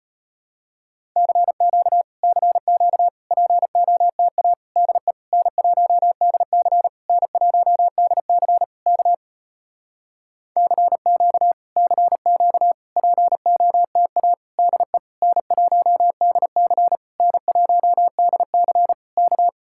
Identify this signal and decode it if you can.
Morse code, heard as a keyed tone, about 33 wpm, 700 Hz. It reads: CQ CQ POTA DE N1DC N1DC K CQ CQ POTA DE N1DC N1DC K